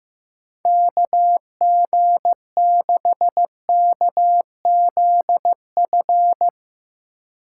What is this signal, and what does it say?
Morse code, heard as a keyed tone, about 16 wpm, 700 Hz. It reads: KG6KZF
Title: Speed Racing - USA Call Signs 15wpm